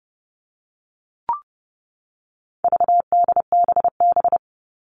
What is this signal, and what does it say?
Morse code, heard as a keyed tone, about 30 wpm, 700 Hz. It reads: VDBB
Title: All Cut Numbers - 4-Digit 30wpm